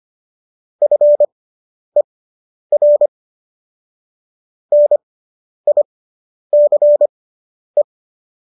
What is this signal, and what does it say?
Morse code, heard as a keyed tone, about 25 wpm, spaced as for 10 wpm, 590 Hz. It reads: FER NICE